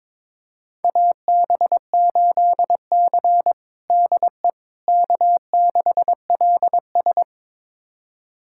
Morse code, heard as a keyed tone, about 22 wpm, 700 Hz. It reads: AB8C DE K6LS
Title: USA Callsign DE Callsign - 4 Characters - 22wpm